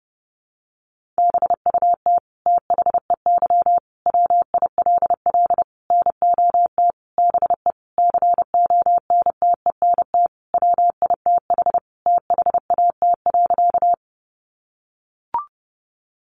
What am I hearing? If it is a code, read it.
Morse code, heard as a keyed tone, about 30 wpm, 700 Hz. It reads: BUT THEY WILL NOT BE CONTENT WITH THAT.
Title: Sentences from Top 900 Words 30wpm